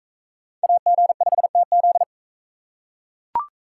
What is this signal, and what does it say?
Morse code, heard as a keyed tone, about 42 wpm, 700 Hz. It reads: AG5TZ